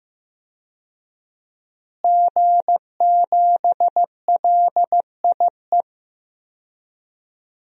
Morse code, heard as a keyed tone, about 15 wpm, 700 Hz. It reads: G7LIE
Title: DXCC-100 Call Signs - 5 Characters - Repeated 2x - 15wpm